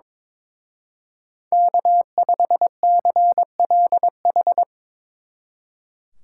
Morse code, heard as a keyed tone, about 25 wpm, 700 Hz. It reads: K5CLH
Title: Speed Racing - USA 1x3 Call Signs 22wpm